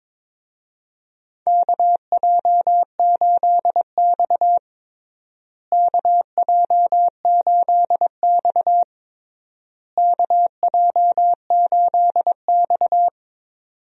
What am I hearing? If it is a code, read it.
Morse code, heard as a keyed tone, about 22 wpm, 700 Hz. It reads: KJ8X KJ8X KJ8X
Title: USA Call Signs - 4 Characters - Repeated 3x - 22wpm